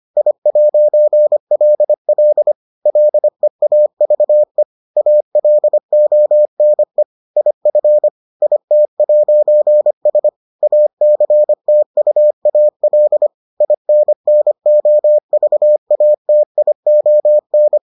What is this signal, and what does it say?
Morse code, heard as a keyed tone, about 25 wpm, 600 Hz. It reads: I'LL LEAVE ALONE IF IT'S ACTUAL INNOVATION